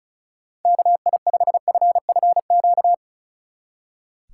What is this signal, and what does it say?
Morse code, heard as a keyed tone, about 45 wpm, 700 Hz. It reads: KI5FFQ